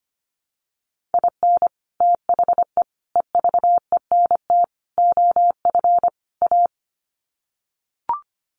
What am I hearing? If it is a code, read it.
Morse code, heard as a keyed tone, about 25 wpm, 700 Hz. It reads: IN THE EVENT OF A
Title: Sets of 5 Words Encore 25wpm